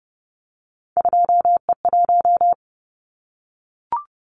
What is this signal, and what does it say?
Morse code, heard as a keyed tone, about 30 wpm, 700 Hz. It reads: 2E1
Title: Callsign Prefixes for Top 25 DXCC - 30wpm